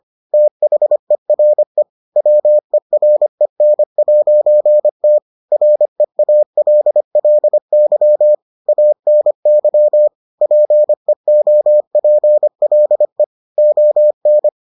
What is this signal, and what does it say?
Morse code, heard as a keyed tone, about 25 wpm, 600 Hz. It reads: THERE WEREN'T REALLY ANY PEOPLE ON